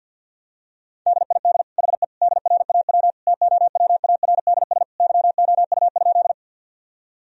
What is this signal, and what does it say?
Morse code, heard as a keyed tone, about 52 wpm, 700 Hz. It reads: DID HE DRAW TOWARDS YOU?
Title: Sentences from Top 1000 Words - Review Part 2 - 50wpm